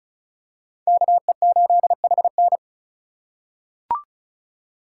Morse code, heard as a keyed tone, about 36 wpm, 700 Hz. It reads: KE8HN
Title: USA Call Signs - 4x Character Spacing - 35wpm